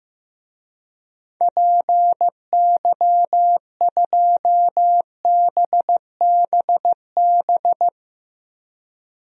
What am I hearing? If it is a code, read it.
Morse code, heard as a keyed tone, about 15 wpm, 700 Hz. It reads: PY2BBB
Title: DXCC-100 Call Signs - 6 Characters - Repeated 2x - 15wpm